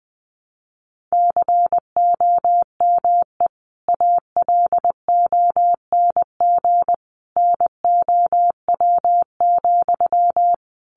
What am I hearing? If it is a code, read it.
Morse code, heard as a keyed tone, about 19 wpm, 700 Hz. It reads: COME ALONG NOW,
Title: Sentences from Top 700 Words 20wpm